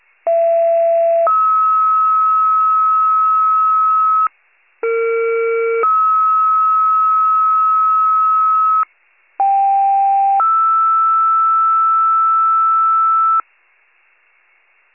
Two-tone page, heard as a one-second tone, then a three-second tone, about 670 then 1290 Hz, about 460 then 1290 Hz, about 790 then 1320 Hz. It is heard from 0.3 to 13.4 s.